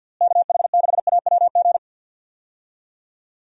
Morse code, heard as a keyed tone, about 47 wpm, 700 Hz. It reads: KH6AWG
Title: DXCC-100 Call Signs - 6 Characters - Repeated 2x - 50wpm